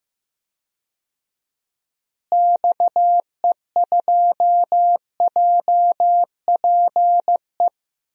Morse code, heard as a keyed tone, about 15 wpm, 700 Hz. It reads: XE2JPE
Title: DXCC-100 Call Signs - 6 Characters - Repeated 2x - 15wpm